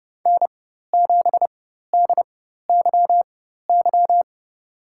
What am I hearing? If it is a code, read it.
Morse code, heard as a keyed tone, about 30 wpm, spaced as for 14 wpm, 700 Hz. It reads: N7DYY
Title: USA Call Signs - 5 Characters - 4x Character Spacing - 30wpm